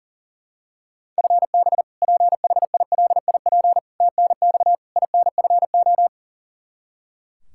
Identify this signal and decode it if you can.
Morse code, heard as a keyed tone, about 40 wpm, 700 Hz. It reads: FB PHILIP TNX INFO
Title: Practice QSO - Exchange 3 - More Info - 40wpm